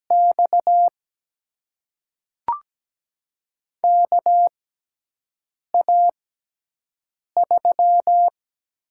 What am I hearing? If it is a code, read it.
Morse code, heard as a keyed tone, about 17 wpm, spaced as for 6 wpm, 700 Hz. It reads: X KA3
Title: USA Call Signs - 6 Characters - 6x Character Spacing - 17wpm